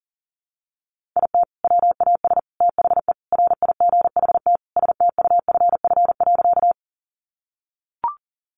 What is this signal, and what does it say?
Morse code, heard as a keyed tone, about 37 wpm, 700 Hz. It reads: IT WAS THE RIGHT STUFF.
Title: Sentences from Top 1400 Words 40wpm